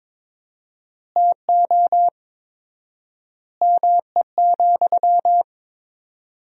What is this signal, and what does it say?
Morse code, heard as a keyed tone, about 22 wpm, 700 Hz. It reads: TO ME,
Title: 4x Word-Spacing - Sentences from Top 500 Words - 22wpm